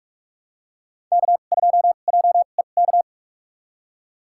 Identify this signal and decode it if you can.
Morse code, heard as a keyed tone, about 45 wpm, spaced as for 32 wpm, 700 Hz. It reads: K2JEK
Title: USA Call Signs - 5 Characters - 2x Character Spacing - 45wpm